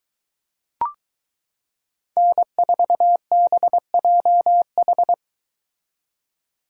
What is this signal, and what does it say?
Morse code, heard as a keyed tone, about 22 wpm, 700 Hz. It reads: N4BJH